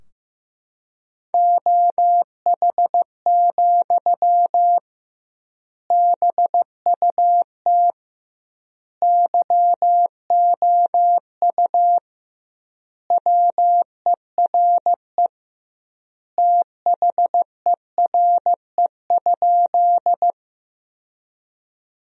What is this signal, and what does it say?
Morse code, heard as a keyed tone, about 15 wpm, 700 Hz. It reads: OH, BUT YOU WERE THERE?
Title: 2x Word-Spacing - Sentences from Top 500 Words - 15wpm